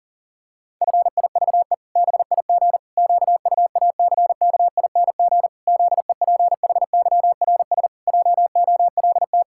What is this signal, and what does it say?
Morse code, heard as a keyed tone, about 40 wpm, 700 Hz. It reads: FIVE BIG QUACKING ZEPHYRS JOLT
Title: Pangram Challenge v2 - 40wpm